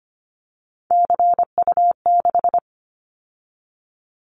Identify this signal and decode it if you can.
Morse code, heard as a keyed tone, about 25 wpm, 700 Hz. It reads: CU6
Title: Callsign Prefixes for Top 50 DXCC - 25wpm